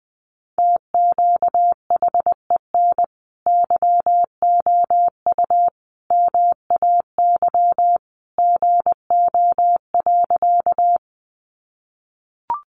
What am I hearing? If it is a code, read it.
Morse code, heard as a keyed tone, about 20 wpm, 700 Hz. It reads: TQHEN YOU MAY GO.
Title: Sentences from Top 100 Words 20wpm